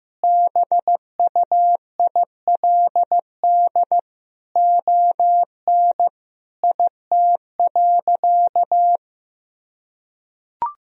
Morse code, heard as a keyed tone, about 15 wpm, 700 Hz. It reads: BUILD ON IT.
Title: Sentences from Top 1000 Words - Review Part 1 - 15wpm